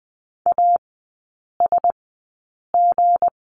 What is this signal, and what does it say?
Morse code, heard as a keyed tone, about 20 wpm, spaced as for 8 wpm, 700 Hz. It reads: ASG
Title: 198 Lesson 32e - Letter k - Three Characters 20wpm